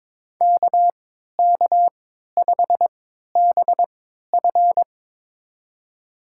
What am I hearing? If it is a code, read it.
Morse code, heard as a keyed tone, about 22 wpm, spaced as for 12 wpm, 700 Hz. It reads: KK5BF
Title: USA Call Signs - 5 Characters - 3x Character Spacing - 22wpm